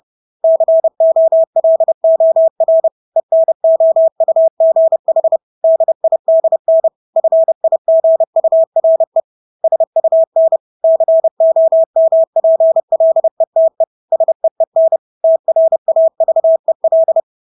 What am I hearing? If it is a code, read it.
Morse code, heard as a keyed tone, about 30 wpm, 650 Hz. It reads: COLOR ENOUGH DIDN FIGURE SUN COMPLETE SEEN TRAVEL
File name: Engelske ord 30-30 wpm